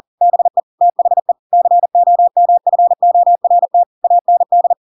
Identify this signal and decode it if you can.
Morse code, heard as a keyed tone, about 40 wpm, 700 Hz. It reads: BE THE COMFORT AND